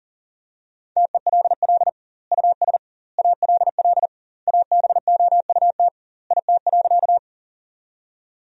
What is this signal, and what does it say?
Morse code, heard as a keyed tone, about 40 wpm, 700 Hz. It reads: TELL US ALL ABOUT IT.
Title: 2x Word-Spacing - Sentences from Top 200 Words - 40wpm